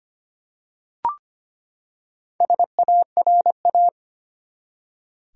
Morse code, heard as a keyed tone, about 25 wpm, 700 Hz. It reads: SARA
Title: USA Common Names - Max 5 Letters - 25wpm